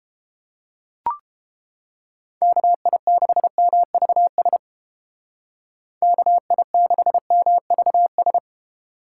Morse code, heard as a keyed tone, about 29 wpm, 700 Hz. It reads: KI6MVS KI6MVS